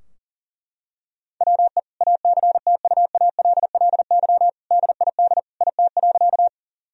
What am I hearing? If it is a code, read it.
Morse code, heard as a keyed tone, about 40 wpm, 700 Hz. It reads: WE ACTUALLY DID IT.
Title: Sentences from Top 1000 Words - Review Part 4 - 40wpm